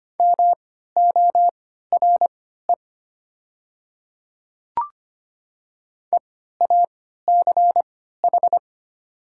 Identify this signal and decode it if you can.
Morse code, heard as a keyed tone, about 25 wpm, spaced as for 14 wpm, 700 Hz. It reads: MORE EACH
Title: 3x Character Spacing - Top 100 Words - 25wpm